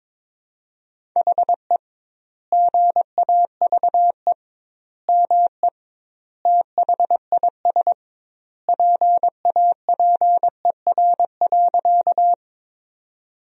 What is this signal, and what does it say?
Morse code, heard as a keyed tone, about 22 wpm, 700 Hz. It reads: HE GAVE ME THIS PAPER.
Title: 2x Word-Spacing - Sentences from Top 500 Words - 22wpm